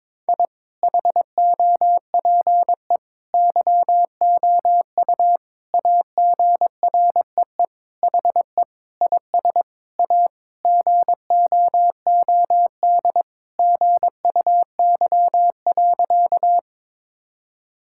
Morse code, heard as a keyed tone, about 22 wpm, 700 Hz. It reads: I HOPE YOU AGREE HE IS A GOOD GUY.
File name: practice / 22wpm / Sentences with 8-Word Phrases - 22wpm